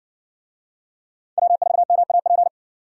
Morse code, heard as a keyed tone, about 58 wpm, 700 Hz. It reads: W4MAP